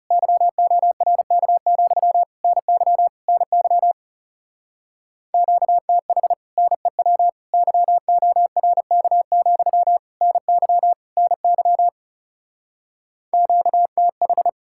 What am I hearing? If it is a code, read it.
Morse code, heard as a keyed tone, about 36 wpm, 700 Hz. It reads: YORK, NY NY QTH NEW YORK, NY NY QTH
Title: Speed-Racing Common QSO Phrases Encore - 30wpm